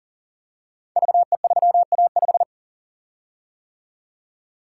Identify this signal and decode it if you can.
Morse code, heard as a keyed tone, about 39 wpm, 700 Hz. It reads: VE3A5
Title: Code Groups - 5 Letter:Numbers - 40wpm